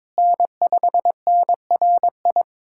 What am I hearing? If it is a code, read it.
Morse code, heard as a keyed tone, about 22 wpm, 700 Hz. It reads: N5NRI
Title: USA Callsign DE Callsign 22wpm